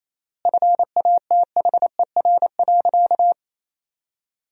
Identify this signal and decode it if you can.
Morse code, heard as a keyed tone, about 28 wpm, 700 Hz. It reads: FATHER.
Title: 3x Word-Spacing - Sentences from Top 500 Words - 28wpm